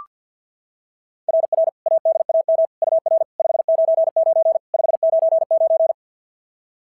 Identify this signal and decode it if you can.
Morse code, heard as a keyed tone, about 45 wpm, 660 Hz. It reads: RR ADAM UR 599 599
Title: SOTA QSO Practice 50wpm